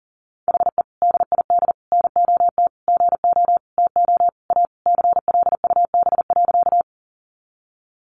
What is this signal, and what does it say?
Morse code, heard as a keyed tone, about 40 wpm, 700 Hz. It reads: HE DID NOT GO TO A CLUB.
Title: Sentences from Top 1200 Words 40wpm